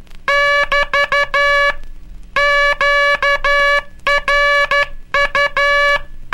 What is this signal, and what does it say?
Morse code, heard as a keyed tone, about 12 wpm, 1700 Hz. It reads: <BT> QRU
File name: cursos / Curso CW mp3 / CW Aula - 18